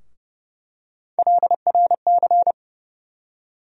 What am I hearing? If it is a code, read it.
Morse code, heard as a keyed tone, about 30 wpm, 700 Hz. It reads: LRC
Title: USA Call Sign Postfixes - 3 Letters 30wpm